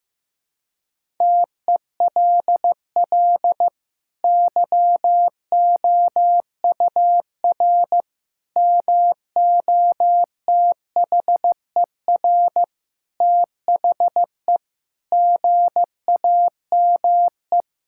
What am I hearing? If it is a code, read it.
Morse code, heard as a keyed tone, about 15 wpm, 700 Hz. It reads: TELL YOUR MOTHER THE GAME
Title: Sentences from Top 1600 Words 15wpm